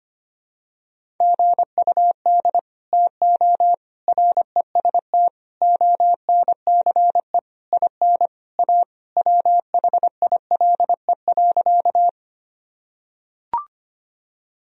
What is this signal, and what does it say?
Morse code, heard as a keyed tone, about 25 wpm, 700 Hz. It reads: GUD TO REST ONCE IN A WHILE.
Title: Sentences from Top 500 Words and Abbreviations 25wpm